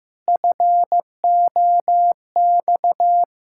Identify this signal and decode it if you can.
Morse code, heard as a keyed tone, about 15 wpm, 700 Hz. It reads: FOX
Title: Pangram Challenge v2 - 15wpm